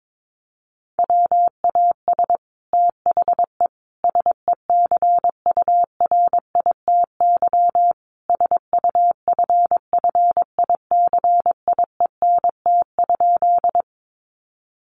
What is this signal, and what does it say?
Morse code, heard as a keyed tone, about 22 wpm, 700 Hz. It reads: WAS THE SECURITY SUFFICIENT?
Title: Sentences from Top 1400 Words 22wpm